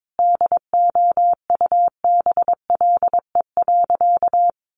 Morse code, heard as a keyed tone, about 22 wpm, 700 Hz. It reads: DOUBLE.
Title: Sentences from Top 1400 Words 22wpm